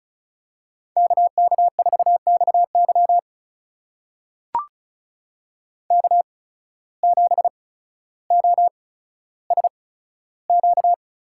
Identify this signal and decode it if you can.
Morse code, heard as a keyed tone, about 34 wpm, spaced as for 15 wpm, 700 Hz. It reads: KK4XY K7OSQ
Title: USA Call Signs - 5 Characters - 8x Character Spacing - 35wpm